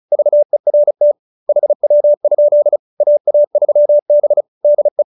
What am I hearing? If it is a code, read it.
Morse code, heard as a keyed tone, about 34 wpm, 590 Hz. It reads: VERT HW? AA3B DE
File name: QSO302 35